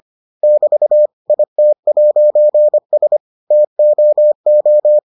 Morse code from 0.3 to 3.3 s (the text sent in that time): <BT> IT'S